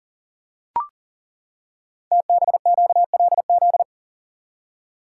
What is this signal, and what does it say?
Morse code, heard as a keyed tone, about 40 wpm, 700 Hz. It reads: TBQLZ
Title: Code Groups - 5 Letter:Numbers - 40wpm